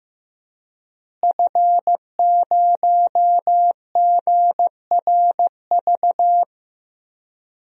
Morse code, heard as a keyed tone, about 15 wpm, 700 Hz. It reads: F0GRV
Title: DXCC-100 Call Signs - 5 Characters - Repeated 2x - 15wpm